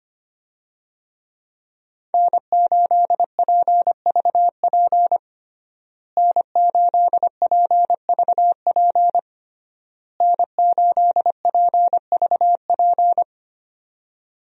Morse code, heard as a keyed tone, about 25 wpm, 700 Hz. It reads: N8PVP N8PVP N8PVP
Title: USA Call Signs - 5 Characters - Repeated 3x - 25wpm